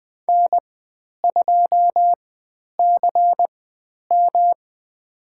Morse code, heard as a keyed tone, about 20 wpm, spaced as for 10 wpm, 700 Hz. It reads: N2CM
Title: 109 Lesson 18g - Letter m - Call Signs 20wpm